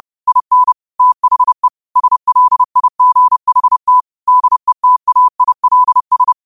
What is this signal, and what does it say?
Morse code, heard as a keyed tone, about 30 wpm, 1000 Hz. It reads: IN THE SLIGHT DETAILS